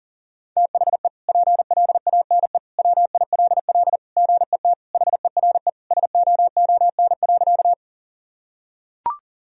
Morse code, heard as a keyed tone, about 40 wpm, 700 Hz. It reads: THE PLANE WILL GET HERE SOON.
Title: Sentences from Top 1000 Words - Review Part 3 - 40wpm